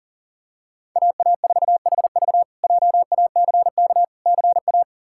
Morse code, heard as a keyed tone, about 40 wpm, 700 Hz. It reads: AA4HV JACK CA
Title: K1USN SST QSO Practice - 40wpm